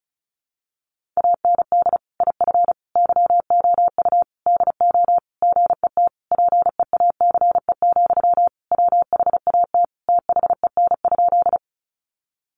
Morse code, heard as a keyed tone, about 34 wpm, 700 Hz. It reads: AND IF YOU DO GET PEACE, WHAT THEN?
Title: Sentences from Top 1500 Words 35wpm